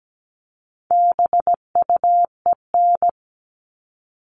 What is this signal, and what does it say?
Morse code, heard as a keyed tone, about 17 wpm, 700 Hz. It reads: BUEN
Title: All Cut Numbers - 4-Digit 17wpm